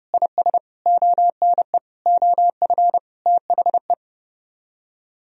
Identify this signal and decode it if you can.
Morse code, heard as a keyed tone, about 40 wpm, spaced as for 35 wpm, 700 Hz. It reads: IS ONE OF THE
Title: Speed-Racing - Sets of 5 Words 30wpm